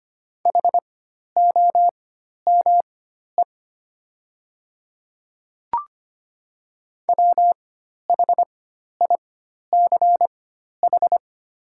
Morse code, heard as a keyed tone, about 25 wpm, spaced as for 12 wpm, 690 Hz. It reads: HOME WHICH
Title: 4x Character Spacing - Top 500 Words - 25wpm